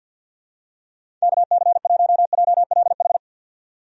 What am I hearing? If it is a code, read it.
Morse code, heard as a keyed tone, about 46 wpm, 700 Hz. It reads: KC1JLH
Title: USA Call Signs - 6 Characters - 50wpm